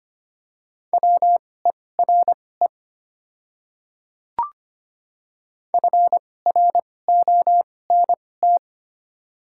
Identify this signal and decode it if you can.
Morse code, heard as a keyed tone, about 25 wpm, spaced as for 18 wpm, 700 Hz. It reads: WERE FRONT
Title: 2x Character Spacing - Top 500 Words - 25wpm